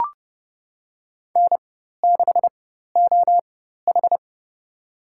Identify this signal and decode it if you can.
Morse code, heard as a keyed tone, about 30 wpm, spaced as for 14 wpm, 700 Hz. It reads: N6OH
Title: USA Call Signs - 4 Characters - 4x Character Spacing - 30wpm